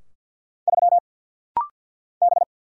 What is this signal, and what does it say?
Morse code, heard as a keyed tone, about 50 wpm, 700 Hz. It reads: 3 B
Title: Single Letter-Number Alt - Rapid-Fire 50wpm